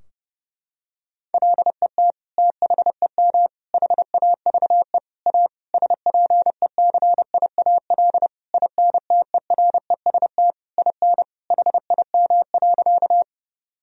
Morse code, heard as a keyed tone, about 30 wpm, 700 Hz. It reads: LET THEM HAVE A SPECIAL INTEREST IN HIM.
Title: Sentences from Top 1000 Words - Review Part 3 - 30wpm